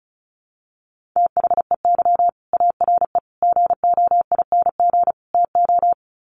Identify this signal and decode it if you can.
Morse code, heard as a keyed tone, about 34 wpm, 700 Hz. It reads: THEY ARE GOING TO
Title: Sets of 4 Words 35wpm